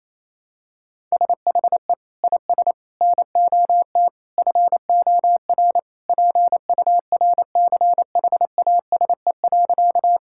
Morse code, heard as a keyed tone, about 28 wpm, 700 Hz. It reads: SHE IS NOT FOR PURCHASE.
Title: Sentences from Top 1000 Words - Review Part 5 - 28wpm